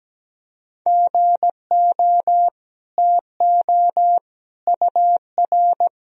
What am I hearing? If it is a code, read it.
Morse code, heard as a keyed tone, about 17 wpm, 700 Hz. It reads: GO TO UR
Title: Sentences from Top 300 Words and Abbreviations 17wpm